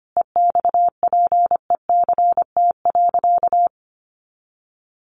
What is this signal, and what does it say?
Morse code, heard as a keyed tone, about 25 wpm, 700 Hz. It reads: EXPECT.
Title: Sentences from Top 1200 Words 25wpm